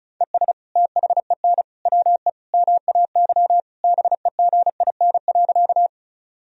Morse code, heard as a keyed tone, about 34 wpm, 700 Hz. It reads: ES THEN WE MAY BEGIN.
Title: Sentences from Top 300 Words and Abbreviations 35wpm